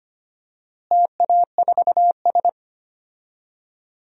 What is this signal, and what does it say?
Morse code, heard as a keyed tone, about 25 wpm, 700 Hz. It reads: TA4S
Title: DXCC-100 Call Signs - 4 Characters - Repeated 3x - 25wpm